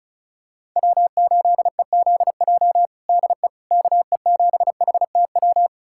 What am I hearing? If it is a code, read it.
Morse code, heard as a keyed tone, about 36 wpm, 700 Hz. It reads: W8EZJ DE KE7HTW
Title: USA Callsign DE Callsign 35wpm